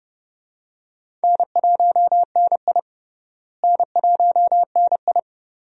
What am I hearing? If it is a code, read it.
Morse code, heard as a keyed tone, about 30 wpm, 700 Hz. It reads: N1NI N1NI
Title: USA Call Signs - 4 Characters - Repeated 2x - 30wpm